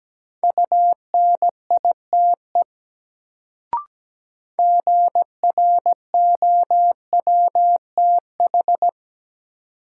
Morse code, heard as a keyed tone, about 17 wpm, 700 Hz. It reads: UNITE GROWTH
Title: Sentences from Top 1000 Words - Review Part 5 - New Word Drill - 17wpm